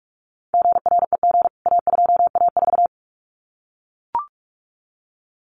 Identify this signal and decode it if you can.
Morse code, heard as a keyed tone, about 44 wpm, 700 Hz. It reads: GREG A2A4
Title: CWT Exchange 45wpm